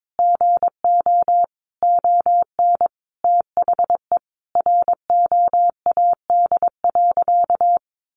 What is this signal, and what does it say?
Morse code, heard as a keyed tone, about 22 wpm, 700 Hz. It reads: GO ON THE ROAD.
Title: Sentences from Top 1400 Words 22wpm